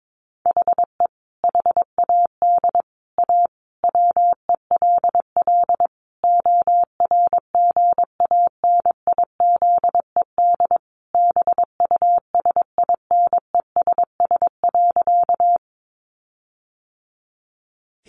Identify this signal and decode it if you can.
Morse code, heard as a keyed tone, about 22 wpm, 700 Hz. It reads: HE HAD A WELL ORGANIZED BUSINESS.
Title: Sentences from Top 1300 Words 22wpm